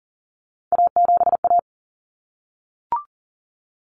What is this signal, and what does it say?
Morse code, heard as a keyed tone, about 40 wpm, 700 Hz. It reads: A7A
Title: Common Cut Numbers - 3-Digit 40wpm